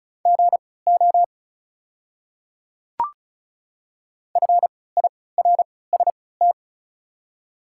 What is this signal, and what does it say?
Morse code, heard as a keyed tone, about 36 wpm, spaced as for 20 wpm, 700 Hz. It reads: GO FIRST
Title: 3x Character Spacing - Top 200 Words - 35wpm